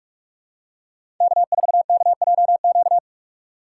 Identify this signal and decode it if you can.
Morse code, heard as a keyed tone, about 46 wpm, 700 Hz. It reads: K4KJQ
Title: USA Call Signs - 5 Characters - 2x Character Spacing - 45wpm